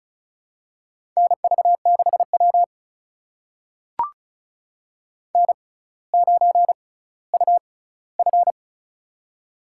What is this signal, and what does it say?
Morse code, heard as a keyed tone, about 36 wpm, spaced as for 18 wpm, 700 Hz. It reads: NV6W N9UF
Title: USA Call Signs - 4 Characters - 6x Character Spacing - 35wpm